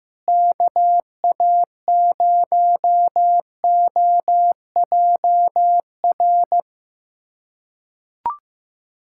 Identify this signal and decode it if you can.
Morse code, heard as a keyed tone, about 15 wpm, 700 Hz. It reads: KA0OJR
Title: USA Callsign DE Callsign 15wpm